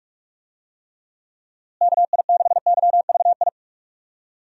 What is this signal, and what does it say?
Morse code, heard as a keyed tone, about 46 wpm, 700 Hz. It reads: KI6YVI
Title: USA Call Signs - 6 Characters - Repeated 2x - 45wpm